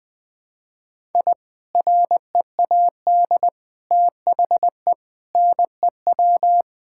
Morse code, heard as a keyed tone, about 20 wpm, 700 Hz. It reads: I READ THE NEW
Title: Sentences with 3-Word Phrases - 20wpm